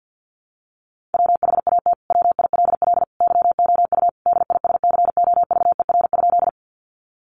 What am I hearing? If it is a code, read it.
Morse code, heard as a keyed tone, about 50 wpm, 700 Hz. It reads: WHAT WILL YOU DISCOVER?
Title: Sentences from Top 1100 Words 50wpm